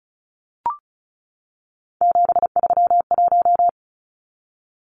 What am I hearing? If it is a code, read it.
Morse code, heard as a keyed tone, about 34 wpm, 700 Hz. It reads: 731
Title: Numbers - 3-Digit 35wpm